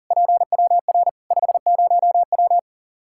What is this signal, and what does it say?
Morse code, heard as a keyed tone, about 40 wpm, 700 Hz. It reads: PWR 50W
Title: K1USN SST QSO Practice 40wpm